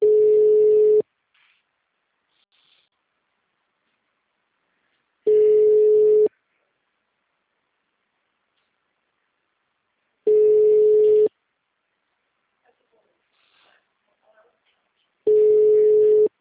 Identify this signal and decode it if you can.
Ringback tone (Europe, 425 Hz).